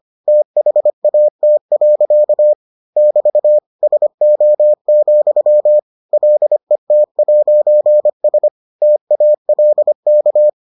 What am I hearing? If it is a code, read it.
Morse code, heard as a keyed tone, about 25 wpm, 600 Hz. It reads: THAT. <BT> SO, LET'S TALK